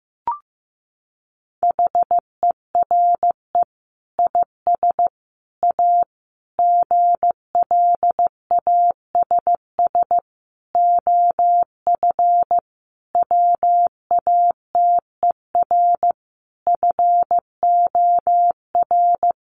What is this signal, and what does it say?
Morse code, heard as a keyed tone, about 15 wpm, 700 Hz. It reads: HERE IS A GLASS OF WATER FOR
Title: Sentences from Top 1300 Words 15wpm